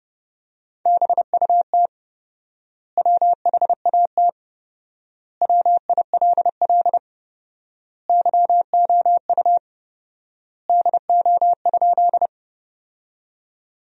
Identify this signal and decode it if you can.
Morse code, heard as a keyed tone, about 32 wpm, 700 Hz. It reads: BUT WHAT WILL YOU DO?
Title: 4x Word-Spacing - Sentences from Top 100 Words - 30wpm